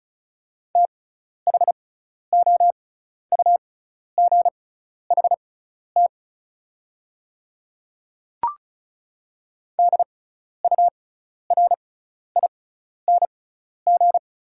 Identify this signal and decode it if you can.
Morse code, heard as a keyed tone, about 36 wpm, spaced as for 12 wpm, 690 Hz. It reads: THOUGHT DURING
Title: 6x Character Spacing - Top 500 Words - 35wpm